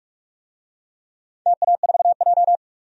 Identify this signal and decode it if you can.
Morse code, heard as a keyed tone, about 46 wpm, 700 Hz. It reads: TA4J
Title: DXCC-100 Call Signs - 4 Characters - 45wpm